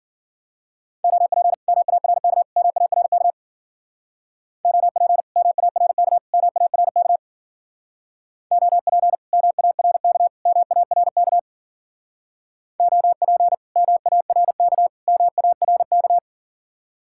Morse code, heard as a keyed tone, about 52 wpm, 700 Hz. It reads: OP MARK MARK OP MARK MARK OP MARK MARK OP MARK MARK